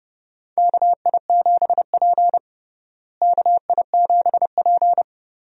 Morse code, heard as a keyed tone, about 29 wpm, 700 Hz. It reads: KI7P KI7P
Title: USA Call Signs - 4 Characters - Repeated 2x - 30wpm